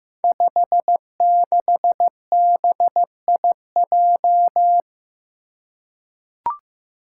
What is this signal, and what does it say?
Morse code, heard as a keyed tone, about 15 wpm, 700 Hz. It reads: 56BIJ
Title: Five Challenging Letter-Number Combinations 15wpm